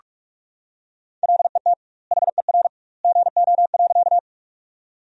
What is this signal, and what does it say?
Morse code, heard as a keyed tone, about 43 wpm, 700 Hz. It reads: LET HER GO.
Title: 2x Word-Spacing - Sentences from Top 300 Words - 45wpm